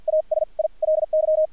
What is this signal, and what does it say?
Morse code, heard as a keyed tone, about 50 wpm, spaced as for 39 wpm, 620 Hz. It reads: ASILY
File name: archived / CRSnet